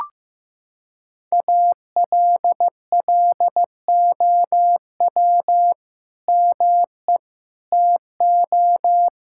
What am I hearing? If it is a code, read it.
Morse code, heard as a keyed tone, about 15 wpm, 700 Hz. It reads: ALLOW ME TO
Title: Sentences from Top 1600 Words 15wpm